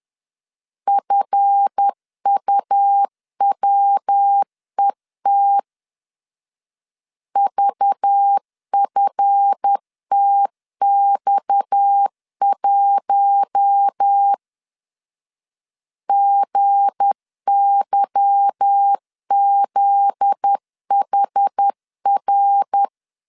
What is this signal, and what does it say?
Morse code, heard as a keyed tone, about 10 wpm, 800 Hz. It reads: FUWET VFTX1 GYZHR
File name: LEZIONI / 8 YJ45 / 8 YJ45 10wpm 2549